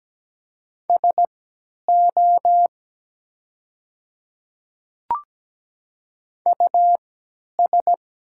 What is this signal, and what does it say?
Morse code, heard as a keyed tone, about 17 wpm, spaced as for 10 wpm, 700 Hz. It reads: SO US
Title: 3x Character Spacing - Top 500 Words - 17wpm